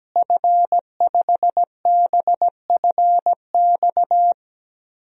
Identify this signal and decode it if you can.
Morse code, heard as a keyed tone, about 17 wpm, 700 Hz. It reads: F5BFX
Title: DXCC-100 Call Signs - 5 Characters - Repeated 3x - 17wpm